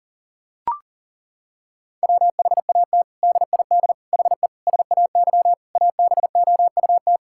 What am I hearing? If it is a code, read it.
Morse code, heard as a keyed tone, about 37 wpm, 700 Hz. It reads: WHAT DID HE SAY ABOUT
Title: Sentences from Top 1000 Words - Review Part 6 - 40wpm